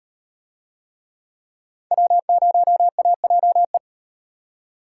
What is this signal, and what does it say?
Morse code, heard as a keyed tone, about 38 wpm, 700 Hz. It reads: W0AJE